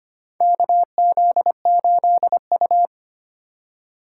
Morse code, heard as a keyed tone, about 25 wpm, 700 Hz. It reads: KZ8U
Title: USA Call Signs - 4 Characters - 4x Character Spacing - 25wpm